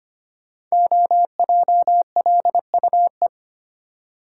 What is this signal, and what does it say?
Morse code, heard as a keyed tone, about 25 wpm, 700 Hz. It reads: OJLUE
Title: Code Groups - 5 Letter:Numbers - 25wpm